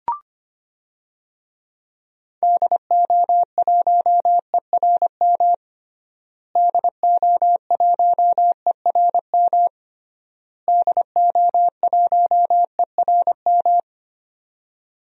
Morse code, heard as a keyed tone, about 25 wpm, 700 Hz. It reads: DO1ERM DO1ERM DO1ERM
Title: DXCC-100 Call Signs - 6 Characters - Repeated 3x - 25wpm